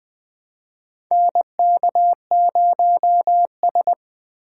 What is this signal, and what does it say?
Morse code, heard as a keyed tone, about 20 wpm, 700 Hz. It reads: NK0S
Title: USA Call Signs - 4x Character Spacing - 20wpm